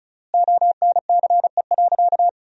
Morse code, heard as a keyed tone, about 34 wpm, 700 Hz. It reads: ONCE.
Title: 4x Word-Spacing - Sentences from Top 500 Words - 35wpm